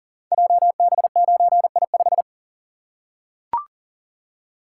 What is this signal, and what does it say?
Morse code, heard as a keyed tone, about 38 wpm, 690 Hz. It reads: JB9I5
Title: Code Groups - 5 Letter:Numbers - 40wpm